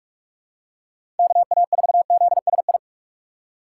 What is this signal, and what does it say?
Morse code, heard as a keyed tone, about 41 wpm, 700 Hz. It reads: KA4ZSI
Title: USA Call Signs - 6 Characters - 4x Character Spacing - 45wpm